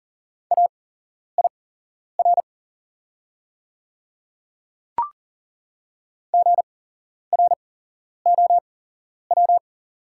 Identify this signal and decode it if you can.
Morse code, heard as a keyed tone, about 40 wpm, spaced as for 11 wpm, 700 Hz. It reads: AIR GROW
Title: 8x Character Spacing - Top 500 Words - 40wpm